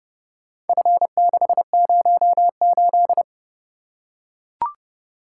Morse code, heard as a keyed tone, about 27 wpm, 700 Hz. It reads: F608